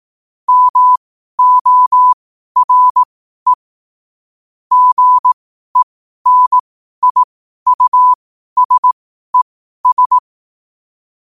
Morse code, heard as a keyed tone, about 18 wpm, spaced as for 12 wpm, 1000 Hz. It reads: MORE GENIUSES
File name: qotdfiles / 12wpm